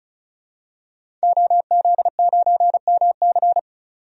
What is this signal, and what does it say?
Morse code, heard as a keyed tone, about 35 wpm, 700 Hz. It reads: OZ9MC
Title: DXCC-100 Call Signs - 5 Characters - 35wpm